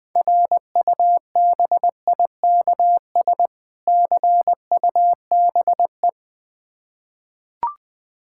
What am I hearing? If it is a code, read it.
Morse code, heard as a keyed tone, about 21 wpm, 700 Hz. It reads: RUBIKS CUBE
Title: Difficult Words with Hints - 20wpm